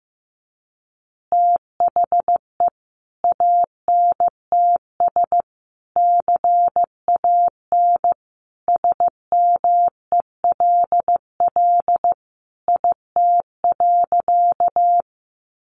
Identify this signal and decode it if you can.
Morse code, heard as a keyed tone, about 15 wpm, 700 Hz. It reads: THE ANTS CAN SMELL IT.
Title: Easy Phrases 15wpm